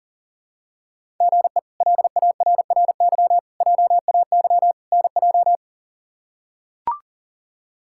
Morse code, heard as a keyed tone, about 37 wpm, 700 Hz. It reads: GE LARRY JAY NJ
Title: K1USN SST QSO Practice - 40wpm